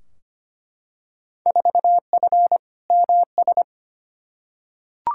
Morse code, heard as a keyed tone, about 25 wpm, 700 Hz. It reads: 4F MS
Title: Field Day Exchanges Encore - 25wpm